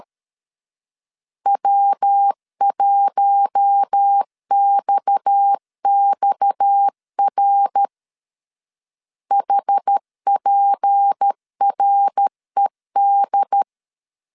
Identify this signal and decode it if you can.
Morse code, heard as a keyed tone, about 13 wpm, 800 Hz. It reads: W1XXR HPRED
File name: LEZIONI / 6 PKX0 / 6 PKX 12wpm 2024